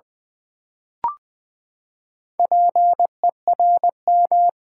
Morse code, heard as a keyed tone, about 20 wpm, 700 Hz. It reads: PERM
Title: Build A Word - 4 Letters At A Time - 11-Letter Words - 20wpm